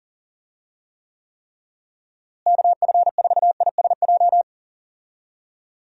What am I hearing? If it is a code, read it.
Morse code, heard as a keyed tone, about 40 wpm, 700 Hz. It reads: KF4ISJ
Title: USA Call Signs - 6 Characters - Repeated 3x - 40wpm